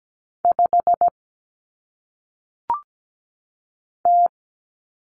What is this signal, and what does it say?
Morse code, heard as a keyed tone, about 17 wpm, 700 Hz. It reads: 5 T
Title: Common Cut Numbers - 1-Digit 17wpm